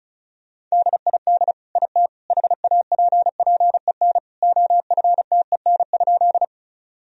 Morse code, heard as a keyed tone, about 34 wpm, 700 Hz. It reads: DID IT HAPPEN OFTEN?
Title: Sentences from Top 1000 Words - Review Part 1 - 35wpm